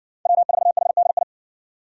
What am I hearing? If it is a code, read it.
Morse code, heard as a keyed tone, about 60 wpm, 700 Hz. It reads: W4HDI